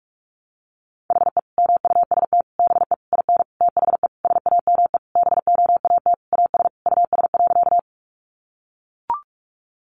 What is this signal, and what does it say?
Morse code, heard as a keyed tone, about 44 wpm, 700 Hz. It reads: HE MUST BE IN THE SAME BOAT AS US.
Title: Sentences from Top 500 Words 45wpm